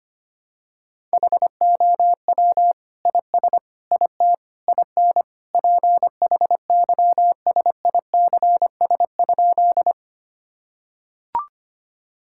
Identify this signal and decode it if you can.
Morse code, heard as a keyed tone, about 25 wpm, 700 Hz. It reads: HOW IS IT IN PHYSICS?
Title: Sentences from Top 1000 Words - Review Part 5 - 25wpm